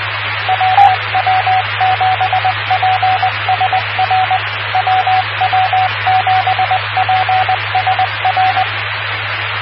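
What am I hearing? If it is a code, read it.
Morse code, heard as a keyed tone, about 20 wpm, 750 Hz. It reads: WW7PSR WW7PSR
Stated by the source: WW7PSR repeater ident on 146.960 MHz, PSRG.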